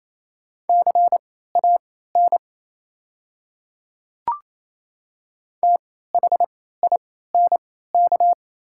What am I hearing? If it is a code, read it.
Morse code, heard as a keyed tone, about 28 wpm, spaced as for 16 wpm, 690 Hz. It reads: CAN THINK
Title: 3x Character Spacing - Top 200 Words - 28wpm